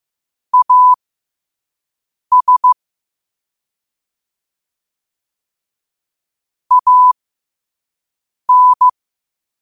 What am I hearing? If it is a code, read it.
Morse code, heard as a keyed tone, about 15 wpm, spaced as for 5 wpm, 1000 Hz. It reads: AS AN
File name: qotdfiles / 5wpm